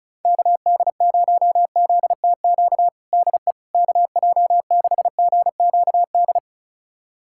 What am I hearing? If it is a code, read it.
Morse code, heard as a keyed tone, about 36 wpm, 700 Hz. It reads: KD0ZTQ DE KJ6GQD
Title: USA Callsign DE Callsign 35wpm